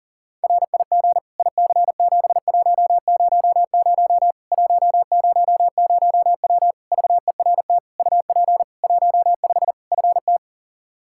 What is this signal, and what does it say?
Morse code, heard as a keyed tone, about 40 wpm, 700 Hz. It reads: RIG IC7100 100W VERT UP 15 FT
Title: K1USN SST QSO Practice 40wpm